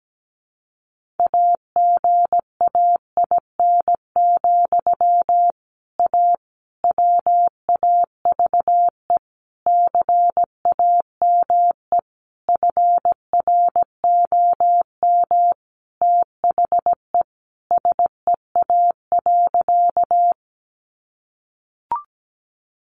Morse code, heard as a keyed tone, about 17 wpm, 700 Hz. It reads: AGAIN, A WAVE CAME FROM THE SEA.
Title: Sentences from Top 1400 Words 17wpm